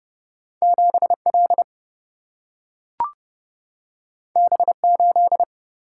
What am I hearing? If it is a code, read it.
Morse code, heard as a keyed tone, about 30 wpm, 700 Hz. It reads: 7L B8
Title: Single-Letter Single-Number 30wpm